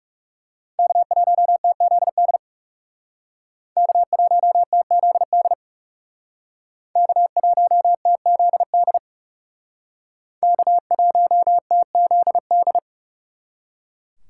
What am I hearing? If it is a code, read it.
Morse code, heard as a keyed tone, about 40 wpm, 700 Hz. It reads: K1TZD K1TZD K1TZD K1TZD